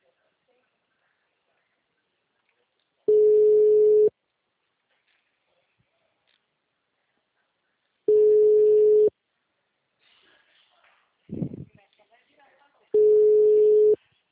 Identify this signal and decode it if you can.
Ringback tone (Europe, 425 Hz), with speech.